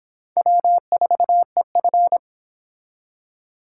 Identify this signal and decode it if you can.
Morse code, heard as a keyed tone, about 23 wpm, 700 Hz. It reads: W4EF